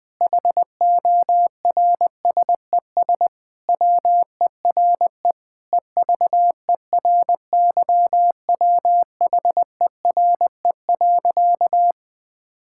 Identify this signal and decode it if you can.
Morse code, heard as a keyed tone, about 20 wpm, 700 Hz. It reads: HORSES WERE EVERYWHERE.
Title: Sentences with 8-Word Phrases - 20wpm